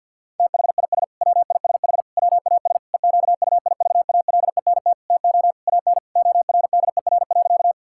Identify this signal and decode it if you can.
Morse code, heard as a keyed tone, about 50 wpm, 700 Hz. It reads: THIS WISH WAS EQUIVALENT TO AN ORDER.
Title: Sentences from Top 1600 Words 50wpm